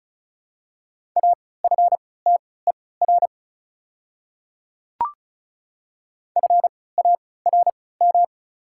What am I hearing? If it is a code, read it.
Morse code, heard as a keyed tone, about 36 wpm, spaced as for 20 wpm, 690 Hz. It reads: AFTER FARM
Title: 3x Character Spacing - Top 500 Words - 35wpm